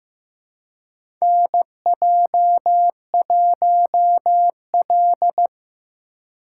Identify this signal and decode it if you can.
Morse code, heard as a keyed tone, about 15 wpm, 700 Hz. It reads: NJ1L
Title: USA Call Signs - 8x Character Spacing - 15wpm